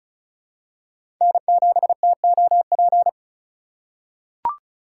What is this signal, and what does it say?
Morse code, heard as a keyed tone, about 36 wpm, 700 Hz. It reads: N7TOP
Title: USA Call Signs - 5 Characters - 2x Character Spacing - 35wpm